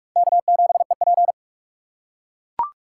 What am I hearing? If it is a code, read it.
Morse code, heard as a keyed tone, about 40 wpm, 700 Hz. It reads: K7EP